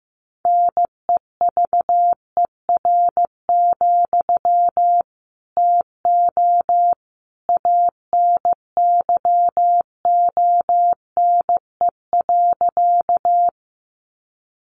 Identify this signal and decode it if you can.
Morse code, heard as a keyed tone, about 15 wpm, 700 Hz. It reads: NEVER, TO ANYONE.
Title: Sentences from Top 900 Words 15wpm